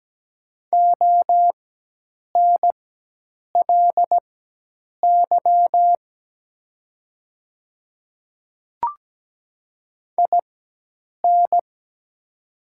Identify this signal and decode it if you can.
Morse code, heard as a keyed tone, about 17 wpm, spaced as for 8 wpm, 700 Hz. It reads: ONLY IN
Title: 4x Character Spacing - Top 300 Words - 17wpm